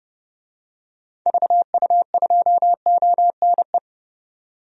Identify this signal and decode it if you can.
Morse code, heard as a keyed tone, about 31 wpm, 700 Hz. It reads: VU2ONE
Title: DXCC-100 Call Signs - 6 Characters - 30wpm